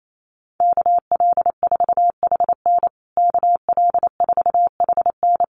Morse code, heard as a keyed tone, about 28 wpm, 700 Hz. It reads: KL4HN KL4HN
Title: USA Call Signs Repeated 3x 28wpm